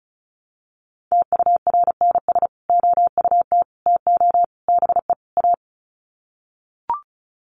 Morse code, heard as a keyed tone, about 34 wpm, 700 Hz. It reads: TURNS OUT TO BE A
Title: Sets of 5 Words Encore 35wpm